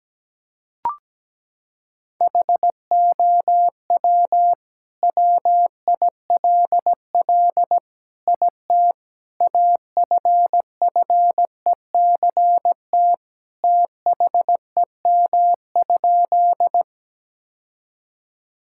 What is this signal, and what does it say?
Morse code, heard as a keyed tone, about 17 wpm, 700 Hz. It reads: HOW WILL IT AFFECT THEM?
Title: Sentences from Top 1000 Words - Review Part 6 - 17wpm